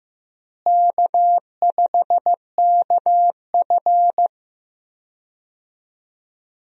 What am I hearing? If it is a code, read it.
Morse code, heard as a keyed tone, about 15 wpm, 700 Hz. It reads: K5KF
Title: USA Call Signs - 4 Characters - Repeated 3x - 15wpm